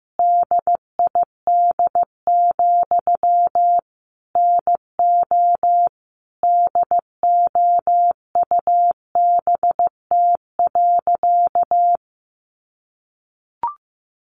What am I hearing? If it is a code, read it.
Morse code, heard as a keyed tone, about 15 wpm, 700 Hz. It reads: DID, NO DOUBT.
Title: Sentences from Top 1500 Words 15wpm